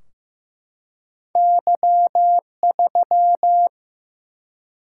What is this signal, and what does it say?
Morse code, heard as a keyed tone, about 15 wpm, 700 Hz. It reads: Y3
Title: Single-Letter Single-Number 15wpm